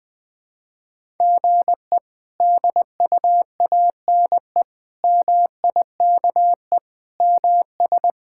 Morse code, heard as a keyed tone, about 21 wpm, 700 Hz. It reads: GE DUANE MIKE MS
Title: K1USN SST QSO Practice - 20wpm